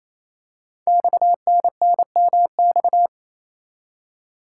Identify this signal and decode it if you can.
Morse code, heard as a keyed tone, about 27 wpm, 700 Hz. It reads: XNNMX
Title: Code Groups - 5 Letters - 28wpm